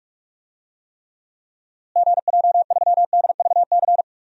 Morse code, heard as a keyed tone, about 46 wpm, 700 Hz. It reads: GJ3DVC
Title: DXCC-100 Call Signs - 6 Characters - Repeated 2x - 45wpm